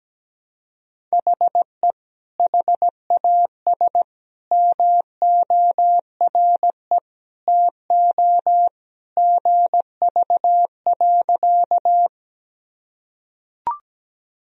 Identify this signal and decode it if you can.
Morse code, heard as a keyed tone, about 17 wpm, 700 Hz. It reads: HE HAS MORE TO GV.
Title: Sentences from Top 200 Words and Abbreviations 17wpm